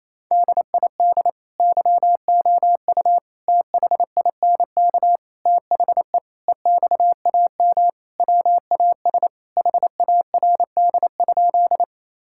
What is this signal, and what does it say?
Morse code, heard as a keyed tone, about 28 wpm, 700 Hz. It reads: DID YOU THINK THE EXAM WAS HARD?
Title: Sentences from Top 1000 Words - Review Part 5 - 28wpm